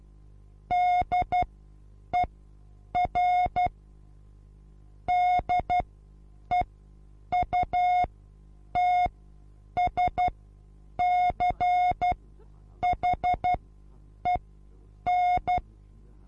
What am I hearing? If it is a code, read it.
Morse code, heard as a keyed tone, about 12 wpm, spaced as for 8 wpm, 720 Hz. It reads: DER DEUTSCHEN
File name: Onlinelehrgang / morsen / DJ4UF Morsen23